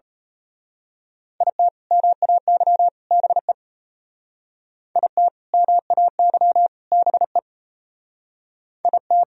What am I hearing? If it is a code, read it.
Morse code, heard as a keyed tone, about 32 wpm, 700 Hz. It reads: IT MAY BE IT MAY BE IT
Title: Speed-Racing - Sets of 3 Words 28wpm